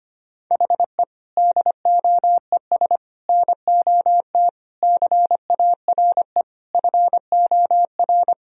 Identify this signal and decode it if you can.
Morse code, heard as a keyed tone, about 25 wpm, 700 Hz. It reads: HE DOES NOT CARE FOR
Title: Sentences from Top 1000 Words - Review Part 1 - 25wpm